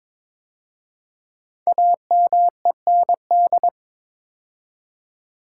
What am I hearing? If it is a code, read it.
Morse code, heard as a keyed tone, about 23 wpm, 700 Hz. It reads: AMEND
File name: 5 Letter Words - First 3 Letters Given - 22wpm